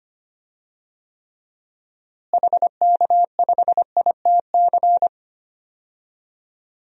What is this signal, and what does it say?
Morse code, heard as a keyed tone, about 25 wpm, 700 Hz. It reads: HK5ITC
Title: DXCC-100 Call Signs - 6 Characters - Repeated 2x - 25wpm